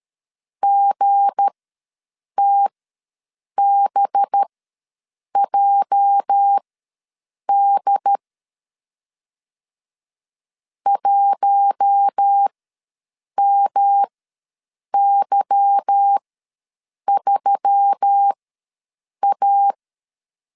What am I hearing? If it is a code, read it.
Morse code, heard as a keyed tone, about 13 wpm, spaced as for 7 wpm, 800 Hz. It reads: GTBJD 1MY3A
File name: LEZIONI / 9 2 ws 12wpm 3406